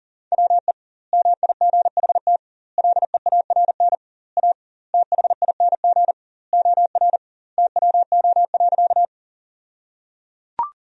Morse code, heard as a keyed tone, about 37 wpm, 700 Hz. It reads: WE MIGHT LEARN A THING OR TWO.
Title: 2x Word-Spacing - Sentences from Top 300 Words - 40wpm